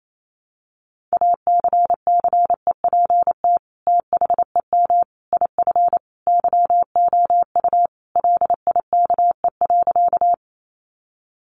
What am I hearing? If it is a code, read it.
Morse code, heard as a keyed tone, about 28 wpm, 700 Hz. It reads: ACCEPT THEM IF YOU LIKE.
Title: Sentences from Top 1300 Words 28wpm